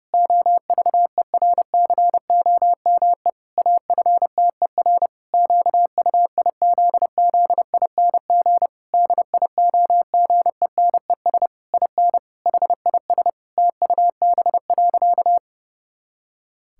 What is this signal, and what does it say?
Morse code, heard as a keyed tone, about 30 wpm, 700 Hz. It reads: OVERCOME AFTER QUIZZING DIOGENES IN HIS TUB.
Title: Pangram Challenge v2 - 30wpm